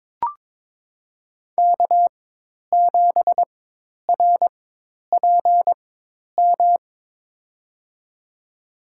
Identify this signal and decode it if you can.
Morse code, heard as a keyed tone, about 22 wpm, spaced as for 10 wpm, 700 Hz. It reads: K7RPM
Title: USA Call Signs - 5 Characters - 4x Character Spacing - 22wpm